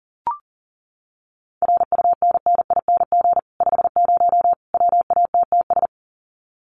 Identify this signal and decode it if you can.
Morse code, heard as a keyed tone, about 40 wpm, 700 Hz. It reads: RUNNING 50 WATTS
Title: Common QSO Phrases 40wpm